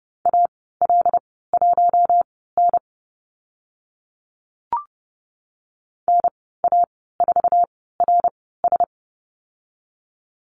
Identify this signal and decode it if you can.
Morse code, heard as a keyed tone, about 30 wpm, spaced as for 17 wpm, 700 Hz. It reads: AL1N NA4RS
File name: Lesson 15g - Number 5 - Call Signs 30wpm